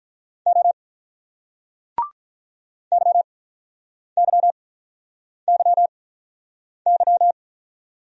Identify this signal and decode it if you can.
Morse code, heard as a keyed tone, about 46 wpm, 700 Hz. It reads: O Y Y Y Y